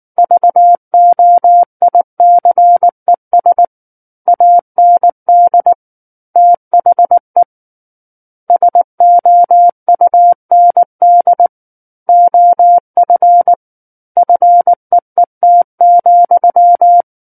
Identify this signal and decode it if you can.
Morse code, heard as a keyed tone, about 19 wpm, 700 Hz. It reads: VOICES AND THE SOUND OF FEET,
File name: Book One / 10-Chapter 10 - 19 WPM